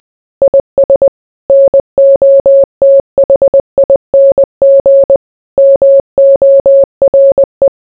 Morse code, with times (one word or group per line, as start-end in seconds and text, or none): 0.4-1.1 IS
1.5-5.2 NOTHING
5.6-7.7 MORE